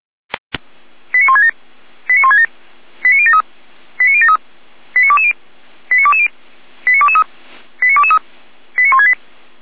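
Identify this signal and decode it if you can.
Five-tone selective call (ZVEI), heard as quick runs of five tones, tones of about 70 ms. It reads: 89167 89167 89073 89073 89220 89220 89223 89223 89168
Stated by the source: ZVEI five-tone selective call 89022, received by SDR.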